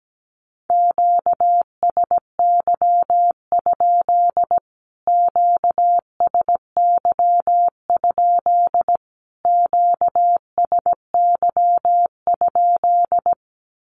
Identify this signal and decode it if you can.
Morse code, heard as a keyed tone, about 17 wpm, 700 Hz. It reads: QSY? QSY? QSY?
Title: Q-Codes Repeated 3x 17wpm